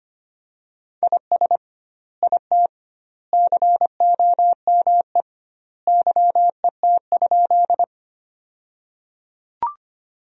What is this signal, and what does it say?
Morse code, heard as a keyed tone, about 25 wpm, 700 Hz. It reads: IS IT COME YET?
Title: 2x Word-Spacing - Sentences from Top 500 Words - 25wpm